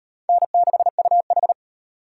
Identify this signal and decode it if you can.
Morse code, heard as a keyed tone, about 35 wpm, 700 Hz. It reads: N6UH